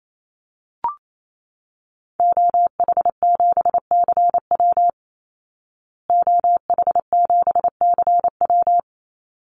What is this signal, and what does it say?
Morse code, heard as a keyed tone, about 28 wpm, 700 Hz. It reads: OH7CW OH7CW
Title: Call Signs Repeated 3x - 28wpm